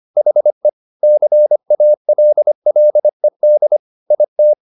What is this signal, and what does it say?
Morse code, heard as a keyed tone, about 25 wpm, 600 Hz. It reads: HE CALLED IT